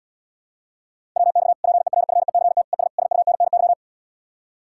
Morse code, heard as a keyed tone, about 77 wpm, 700 Hz. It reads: WX PERFECT ES SUNNY